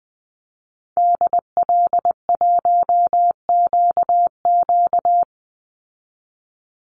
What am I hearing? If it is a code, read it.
Morse code, heard as a keyed tone, about 19 wpm, 700 Hz. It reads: DL1QQ
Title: Call Signs Repeated 2x - 20wpm